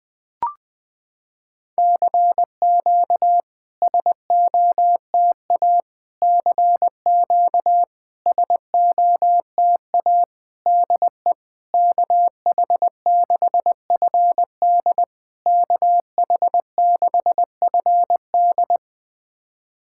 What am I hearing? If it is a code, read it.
Morse code, heard as a keyed tone, about 20 wpm, 700 Hz. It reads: CQ SOTA CQ SOTA DE KH6FD KH6FD
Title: SOTA QSO Practice 20wpm